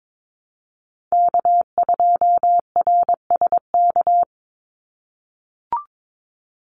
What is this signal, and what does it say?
Morse code, heard as a keyed tone, about 22 wpm, 700 Hz. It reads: K2RSK
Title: Call Signs - 22wpm